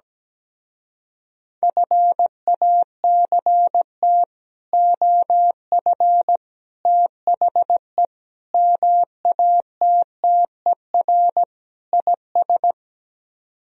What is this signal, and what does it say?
Morse code, heard as a keyed tone, about 17 wpm, 700 Hz. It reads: FACT OF THE MATTER IS
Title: Speed-Racing - Sets of 5 Words 17wpm